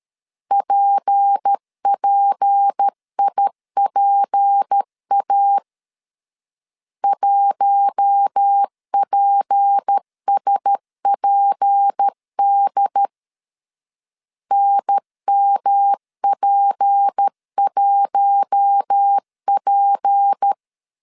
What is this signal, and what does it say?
Morse code, heard as a keyed tone, about 13 wpm, 800 Hz. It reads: PPIPA 1PSPD NMP1P
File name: LEZIONI / 6 PKX0 / 6 P 12wpm 2033